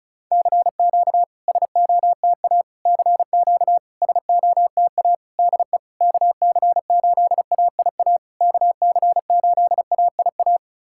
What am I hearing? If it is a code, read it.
Morse code, heard as a keyed tone, about 34 wpm, 700 Hz. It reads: CQ SOTA CQ SOTA DE KC8AIA KC8AIA
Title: SOTA QSO Practice 35wpm